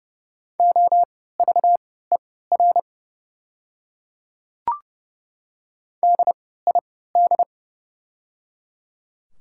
Morse code, heard as a keyed tone, about 30 wpm, spaced as for 17 wpm, 700 Hz. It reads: OVER DID
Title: 3x Character Spacing - Top 200 Words - 30wpm